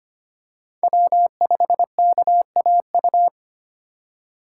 Morse code, heard as a keyed tone, about 25 wpm, 700 Hz. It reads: W5KAU
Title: USA Call Signs - 2x Character Spacing - 25wpm